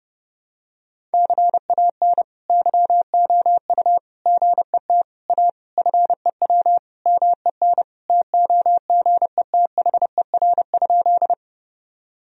Morse code, heard as a keyed tone, about 30 wpm, 700 Hz. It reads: CAN YOU GET A FEW MEN TOGETHER?
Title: Sentences from Top 1000 Words - Review Part 1 - 30wpm